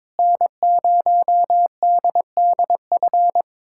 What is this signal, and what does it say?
Morse code, heard as a keyed tone, about 22 wpm, 700 Hz. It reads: N0DDF
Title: USA Callsign DE Callsign 22wpm